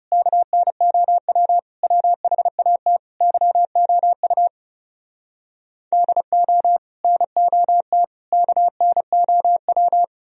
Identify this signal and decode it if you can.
Morse code, heard as a keyed tone, about 37 wpm, 700 Hz. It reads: KNOW WHAT YOU DO NOT KNOW
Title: Speed-Racing - Sets of 5 Words 30wpm